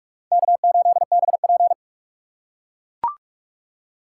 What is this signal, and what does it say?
Morse code, heard as a keyed tone, about 42 wpm, 700 Hz. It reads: K8BP
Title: USA Call Signs - 3x Character Spacing - 45wpm